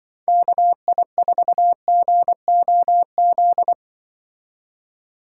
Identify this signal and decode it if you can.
Morse code, heard as a keyed tone, about 25 wpm, 700 Hz. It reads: KI4GOZ
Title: Speed Racing - USA 2x3 Call Signs 20wpm